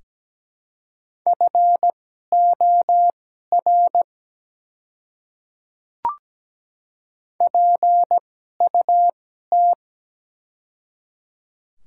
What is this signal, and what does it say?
Morse code, heard as a keyed tone, about 17 wpm, spaced as for 12 wpm, 700 Hz. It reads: FOR PUT
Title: 2x Character Spacing - Top 200 Words - 17wpm